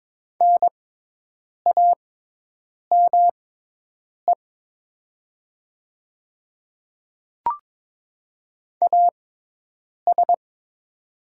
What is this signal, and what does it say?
Morse code, heard as a keyed tone, about 22 wpm, spaced as for 8 wpm, 700 Hz. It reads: NAME AS
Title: 6x Character Spacing - Top 300 Words - 22wpm